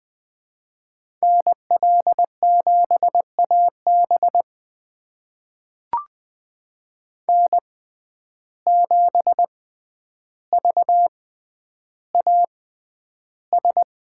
Morse code, heard as a keyed tone, about 20 wpm, spaced as for 10 wpm, 700 Hz. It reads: NL7AB N7VAS
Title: USA Call Signs - 5 Characters - 6x Character Spacing - 20wpm